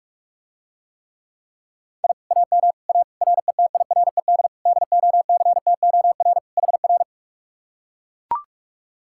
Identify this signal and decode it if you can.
Morse code, heard as a keyed tone, about 44 wpm, 700 Hz. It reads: I AM A RETIRED DOCTOR HR